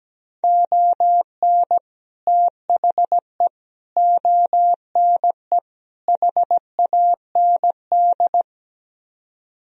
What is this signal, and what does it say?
Morse code, heard as a keyed tone, about 17 wpm, 700 Hz. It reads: ON THE ONE HAND
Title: Speed-Racing - Sets of 4 Words 17wpm